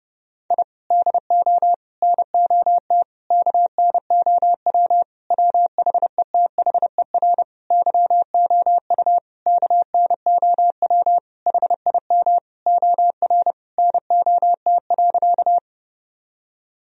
Morse code, heard as a keyed tone, about 32 wpm, 700 Hz. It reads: I DO NOT KNOW WHETHER YOU KNOW HIM OR NOT.
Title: Sentences from Top 1000 Words - Review Part 3 - 30wpm